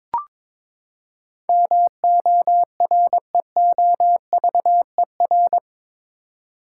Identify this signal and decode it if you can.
Morse code, heard as a keyed tone, about 22 wpm, 700 Hz. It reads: MOREOVER
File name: practice / 22wpm / Eight Letter Words - 22wpm